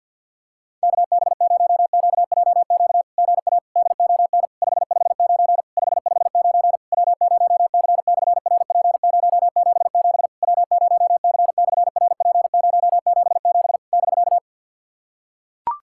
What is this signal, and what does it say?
Morse code, heard as a keyed tone, about 48 wpm, 700 Hz. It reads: KB0QJQ GA DON 559 559 W0C/RP077 W0C/RP077 <BK>
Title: SOTA QSO Practice 50wpm